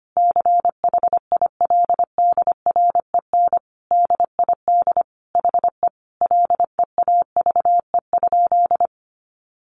Morse code, heard as a keyed tone, about 25 wpm, 700 Hz. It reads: CHILDREN DID HE LEAVE?
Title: Sentences from Top 400 Words 25wpm